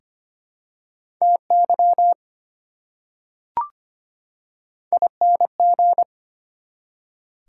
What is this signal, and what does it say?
Morse code, heard as a keyed tone, about 25 wpm, 700 Hz. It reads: TY ING
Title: Prefixes and Suffixes 25wpm